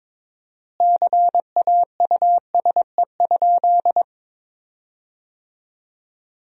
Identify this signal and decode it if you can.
Morse code, heard as a keyed tone, about 22 wpm, 700 Hz. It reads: CAUSE?
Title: 4x Word-Spacing - Sentences from Top 200 Words - 22wpm